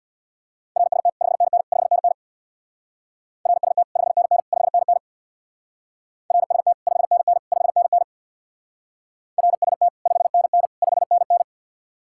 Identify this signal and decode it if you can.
Morse code, heard as a keyed tone, about 64 wpm, 700 Hz. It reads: RST 5NN 5NN RST 5NN 5NN RST 5NN 5NN RST 5NN 5NN